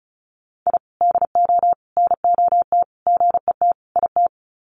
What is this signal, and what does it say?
Morse code, heard as a keyed tone, about 34 wpm, 700 Hz. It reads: I DO NOT GET IT
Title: Sets of 5 Words Encore 35wpm